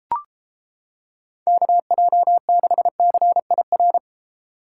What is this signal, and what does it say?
Morse code, heard as a keyed tone, about 30 wpm, 700 Hz. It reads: KJ6CIR